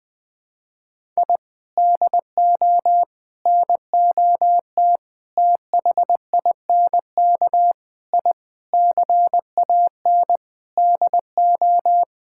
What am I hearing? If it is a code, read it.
Morse code, heard as a keyed tone, about 21 wpm, 700 Hz. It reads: I DO NOT THINK I CAN DO
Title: Sentences with 5-Word Phrases - 20wpm